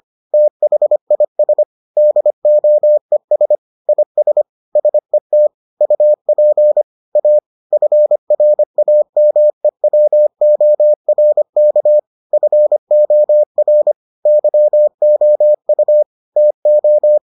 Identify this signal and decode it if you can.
Morse code, heard as a keyed tone, about 25 wpm, 600 Hz. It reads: THIS DOES IS SET UP A FRAMEWORK FOR YOU TO